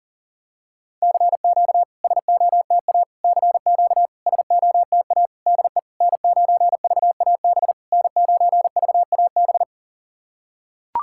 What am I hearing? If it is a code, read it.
Morse code, heard as a keyed tone, about 40 wpm, 690 Hz. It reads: CQ SOTA CQ SOTA DE N9VAB N9VAB
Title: SOTA QSO Practice 40wpm